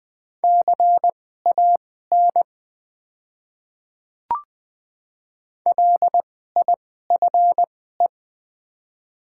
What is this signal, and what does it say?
Morse code, heard as a keyed tone, about 20 wpm, spaced as for 14 wpm, 690 Hz. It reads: CAN LIFE
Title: 2x Character Spacing - Top 500 Words - 20wpm